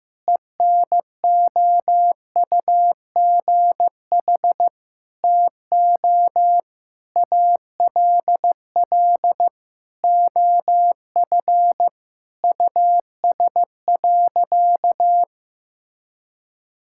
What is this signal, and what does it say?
Morse code, heard as a keyed tone, about 16 wpm, 700 Hz. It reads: ENOUGH TO ALL OF US.
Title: Sentences from Top 1000 Words - Review Part 5 - 15wpm